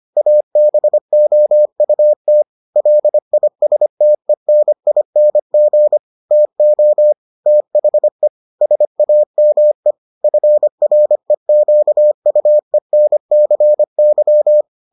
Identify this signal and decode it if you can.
Morse code, heard as a keyed tone, about 25 wpm, 600 Hz. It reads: ABOUT LISTENING TO THE SAME FREQUENCY